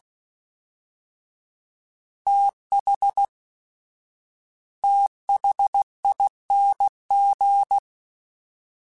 Morse code, heard as a keyed tone, about 16 wpm, 800 Hz. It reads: TH THING
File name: TH / TH mp3s / TH Prefixes 1x1 @15